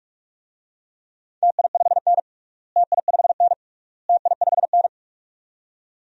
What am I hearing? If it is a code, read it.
Morse code, heard as a keyed tone, about 46 wpm, 700 Hz. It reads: TI5N TI5N TI5N
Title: DXCC-100 Call Signs - 4 Characters - Repeated 3x - 45wpm